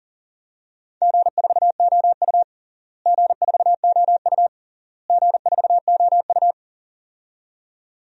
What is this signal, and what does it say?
Morse code, heard as a keyed tone, about 40 wpm, 700 Hz. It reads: G4OU G4OU G4OU
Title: DXCC-100 Call Signs - 4 Characters - Repeated 3x - 40wpm